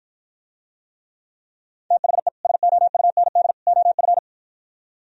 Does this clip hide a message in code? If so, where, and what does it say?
Yes: Morse code from 1.9 to 4.2 s, heard as a keyed tone, about 60 wpm, 700 Hz. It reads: THE SOUND OF